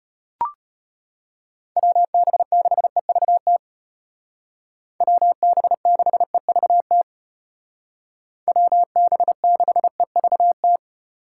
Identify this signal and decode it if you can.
Morse code, heard as a keyed tone, about 30 wpm, 700 Hz. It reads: WB6EVT WB6EVT WB6EVT
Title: Speed Racing - USA 2x3 Call Signs 25wpm